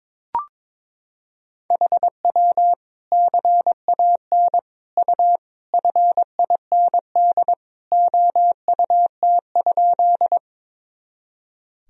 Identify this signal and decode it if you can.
Morse code, heard as a keyed tone, about 22 wpm, 700 Hz. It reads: HW CAN U FIND OUT?
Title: Sentences from Top 200 Words and Abbreviations 22wpm